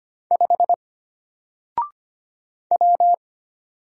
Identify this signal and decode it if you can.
Morse code, heard as a keyed tone, about 25 wpm, 700 Hz. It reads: 5 W
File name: Single-Letter-Number-25wpm